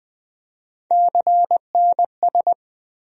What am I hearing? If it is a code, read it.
Morse code, heard as a keyed tone, about 20 wpm, 700 Hz. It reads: CNS
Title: USA Call Sign Postfixes - 3 Letters 20wpm